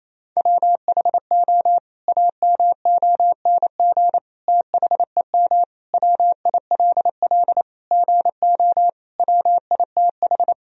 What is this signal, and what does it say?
Morse code, heard as a keyed tone, about 28 wpm, 700 Hz. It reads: WHO AMONG THEM WILL GO WITH
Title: Sentences from Top 1000 Words - Review Part 3 - 28wpm